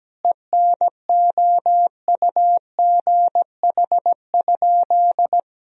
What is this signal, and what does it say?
Morse code, heard as a keyed tone, about 17 wpm, 700 Hz. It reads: ENOUGH?
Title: Sentences from Top 1000 Words - Review Part 4 - 17wpm